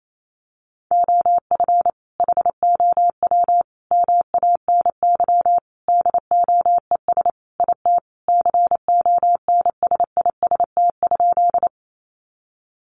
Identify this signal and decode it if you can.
Morse code, heard as a keyed tone, about 28 wpm, 700 Hz. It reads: OF HOW MANY DOES IT CONSIST?
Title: Sentences from Top 1400 Words 28wpm